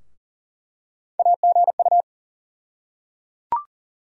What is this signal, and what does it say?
Morse code, heard as a keyed tone, about 38 wpm, 700 Hz. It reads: AGU
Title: USA Call Sign Postfixes - 3 Letters 40wpm